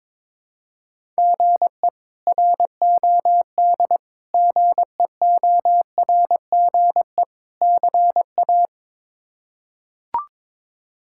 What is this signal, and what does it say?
Morse code, heard as a keyed tone, about 22 wpm, 700 Hz. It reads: GE ROD GEORGE CA
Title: K1USN SST QSO Practice - 22wpm